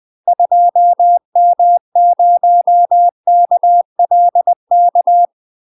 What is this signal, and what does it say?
Morse code, heard as a keyed tone, about 20 wpm, 700 Hz. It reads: 2M0KLK
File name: UK MASTER QSOs TRAINING / 20WPM / UK MASTER QSOs 0053